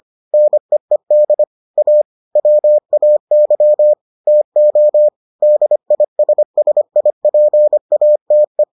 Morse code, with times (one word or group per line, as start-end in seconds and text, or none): 0.3-1.4 NEED
1.8-2.0 A
2.4-3.9 WAY
4.3-5.1 TO
5.4-8.6 DISSIPATE